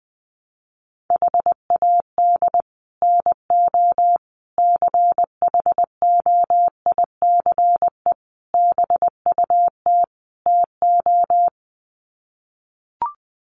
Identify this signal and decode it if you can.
Morse code, heard as a keyed tone, about 20 wpm, 700 Hz. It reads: HAD NO CHOICE BUT TO
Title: Sets of 5 Words 20wpm